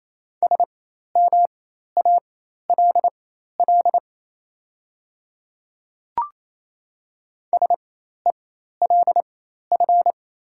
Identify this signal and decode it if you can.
Morse code, heard as a keyed tone, about 28 wpm, spaced as for 13 wpm, 700 Hz. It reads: SMALL SELF
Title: 4x Character Spacing - Top 200 Words - 28wpm